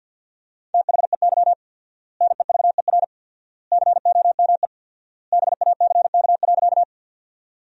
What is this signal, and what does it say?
Morse code, heard as a keyed tone, about 50 wpm, 700 Hz. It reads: THEY NEVER COME BACK.
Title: 4x Word-Spacing - Sentences from Top 300 Words - 50wpm